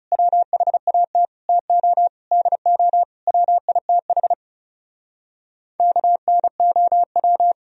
Morse code, heard as a keyed tone, about 33 wpm, 700 Hz. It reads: WHAT TO DO WITH KNOW
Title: Speed-Racing - Sets of 5 Words 30wpm